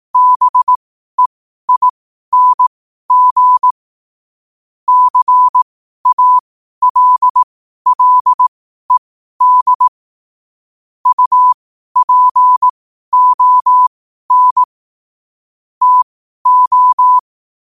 Morse code, heard as a keyed tone, about 18 wpm, spaced as for 12 wpm, 1000 Hz. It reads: BEING CALLED UPON TO
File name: qotdfiles / 12wpm